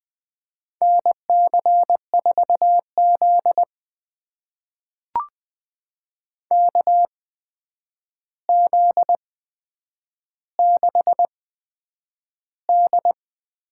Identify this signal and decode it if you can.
Morse code, heard as a keyed tone, about 20 wpm, spaced as for 9 wpm, 700 Hz. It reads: NC4Z KZ6D
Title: USA Call Signs - 4 Characters - 8x Character Spacing - 20wpm